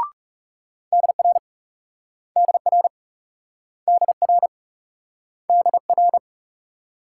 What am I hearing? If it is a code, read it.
Morse code, heard as a keyed tone, about 37 wpm, 700 Hz. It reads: DR DR DR DR
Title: Speed-Racing Multi-letter Phonemes - 30wpm